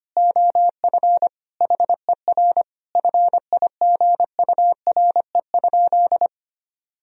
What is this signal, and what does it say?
Morse code, heard as a keyed tone, about 25 wpm, 700 Hz. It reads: OF HER FIGURE?
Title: Sentences from Top 1000 Words - Review Part 2 - 25wpm